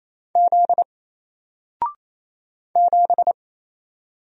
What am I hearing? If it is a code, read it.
Morse code, heard as a keyed tone, about 27 wpm, 700 Hz. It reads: Z 7
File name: Single-Letter-Number-28wpm